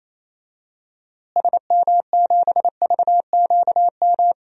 Morse code, heard as a keyed tone, about 28 wpm, 700 Hz. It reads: SM7VQM
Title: DXCC-100 Call Signs - 6 Characters - Repeated 2x - 28wpm